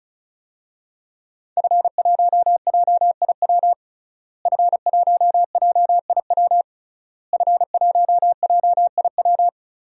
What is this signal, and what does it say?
Morse code, heard as a keyed tone, about 36 wpm, 700 Hz. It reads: F1JIW F1JIW F1JIW
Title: DXCC-100 Call Signs - 5 Characters - Repeated 3x - 35wpm